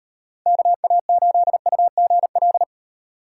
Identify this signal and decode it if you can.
Morse code, heard as a keyed tone, about 38 wpm, 700 Hz. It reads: KA8UGL